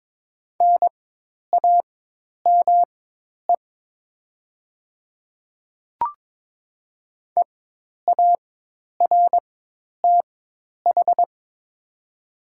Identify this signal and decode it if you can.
Morse code, heard as a keyed tone, about 22 wpm, spaced as for 10 wpm, 690 Hz. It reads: NAME EARTH
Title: 4x Character Spacing - Top 200 Words - 22wpm